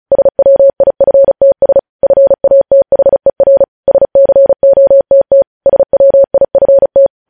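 Morse code, heard as a keyed tone, about 36 wpm, 550 Hz. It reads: SWIFTS FATHER SCOTT SWIFT
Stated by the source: Morse code practice reading news headlines at 35 words per minute.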